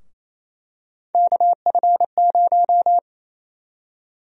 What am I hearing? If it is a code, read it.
Morse code, heard as a keyed tone, about 28 wpm, 700 Hz. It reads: KF0
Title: USA Call Sign Prefixes - 28wpm